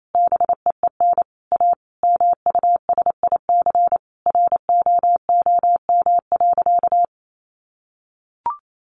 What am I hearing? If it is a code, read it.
Morse code, heard as a keyed tone, about 28 wpm, 700 Hz. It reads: BEEN A MUSIC ROOM.
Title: Sentences from Top 700 Words 28wpm